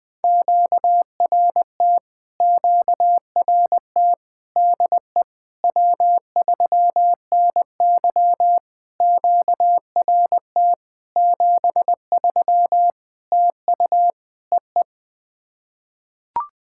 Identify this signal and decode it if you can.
Morse code, heard as a keyed tone, about 21 wpm, 700 Hz. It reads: QRT QRT DE W3NY QRT 73 TU EE
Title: POTA QSO Practice - 20wpm